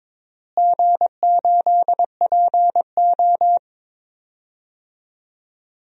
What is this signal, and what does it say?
Morse code, heard as a keyed tone, about 22 wpm, 700 Hz. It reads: G8PO
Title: DXCC-100 Call Signs - 4 Characters - Repeated 3x - 22wpm